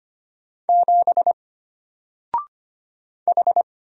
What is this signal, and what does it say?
Morse code, heard as a keyed tone, about 25 wpm, 700 Hz. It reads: 7 H
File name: Single-Letter-Number-25wpm